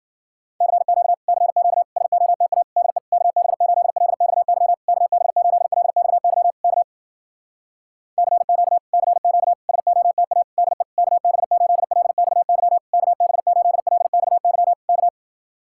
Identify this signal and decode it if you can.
Morse code, heard as a keyed tone, about 52 wpm, 700 Hz. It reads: CQ CQ SOTA DE KB8LXY KB8LXY K CQ CQ SOTA DE KB8LXY KB8LXY K